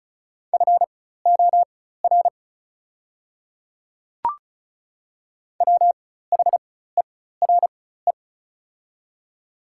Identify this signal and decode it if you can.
Morse code, heard as a keyed tone, about 36 wpm, spaced as for 16 wpm, 690 Hz. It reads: FOR WHERE
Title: 4x Character Spacing - Top 200 Words - 35wpm